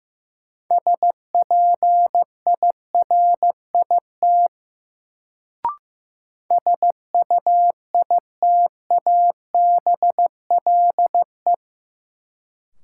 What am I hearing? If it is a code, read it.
Morse code, heard as a keyed tone, about 16 wpm, 700 Hz. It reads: SPIRIT SUITABLE
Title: Sentences from Top 1400 Words - New Word Drill Alt - 15wpm